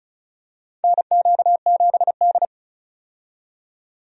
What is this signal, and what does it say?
Morse code, heard as a keyed tone, about 36 wpm, 700 Hz. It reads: NQ7D
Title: USA Call Signs - 4 Characters - 35wpm